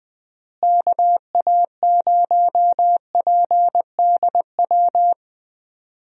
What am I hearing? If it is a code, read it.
Morse code, heard as a keyed tone, about 21 wpm, 700 Hz. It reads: KA0PDW
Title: USA Call Signs - 6 Characters - Repeated 3x - 20wpm